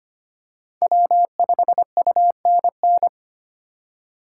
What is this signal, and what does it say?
Morse code, heard as a keyed tone, about 25 wpm, 700 Hz. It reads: W5UNN
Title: USA Call Signs - 5 Characters - 2x Character Spacing - 25wpm